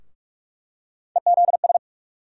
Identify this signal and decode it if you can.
Morse code, heard as a keyed tone, about 46 wpm, 700 Hz. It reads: EZS
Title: USA Call Sign Postfixes - 3 Letters 45wpm